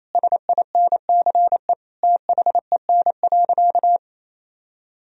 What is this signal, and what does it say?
Morse code, heard as a keyed tone, about 28 wpm, 700 Hz. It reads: SINCE THEN.
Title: Sentences from Top 1000 Words - Review Part 1 - 28wpm